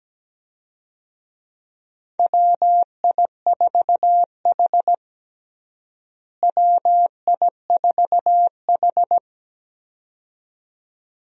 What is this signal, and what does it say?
Morse code, heard as a keyed tone, about 17 wpm, 700 Hz. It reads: WI4H WI4H
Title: USA Call Signs - 4 Characters - Repeated 2x - 17wpm